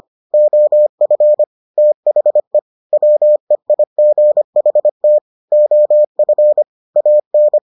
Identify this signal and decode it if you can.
Morse code, heard as a keyed tone, about 25 wpm, 600 Hz. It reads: OF THE WEIGHT OF AN